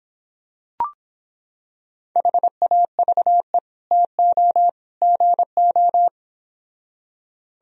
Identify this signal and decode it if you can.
Morse code, heard as a keyed tone, about 26 wpm, 700 Hz. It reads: HAVE TO GO